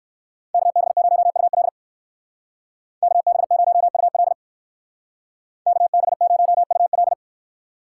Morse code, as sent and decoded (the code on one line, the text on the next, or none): -.- -... ----- ..- .-.. / -.- -... ----- ..- .-.. / -.- -... ----- ..- .-..
KB0UL KB0UL KB0UL